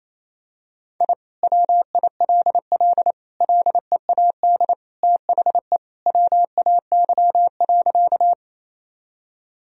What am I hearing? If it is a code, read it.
Morse code, heard as a keyed tone, about 28 wpm, 700 Hz. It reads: I WILL LEAD THE WAY.
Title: Sentences from Top 1000 Words - Review Part 2 - 28wpm